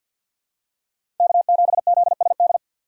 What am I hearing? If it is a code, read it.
Morse code, heard as a keyed tone, about 47 wpm, 700 Hz. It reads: K7ZSD
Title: USA Call Signs - 6x Character Spacing - 50wpm